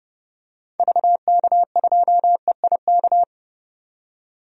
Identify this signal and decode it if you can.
Morse code, heard as a keyed tone, about 30 wpm, 700 Hz. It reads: VK2EIK
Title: DXCC-100 Call Signs - 6 Characters - 30wpm